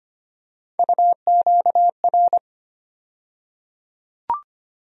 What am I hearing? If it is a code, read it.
Morse code, heard as a keyed tone, about 25 wpm, 700 Hz. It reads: UQR
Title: USA Call Sign Postfixes - 3 Letters 25wpm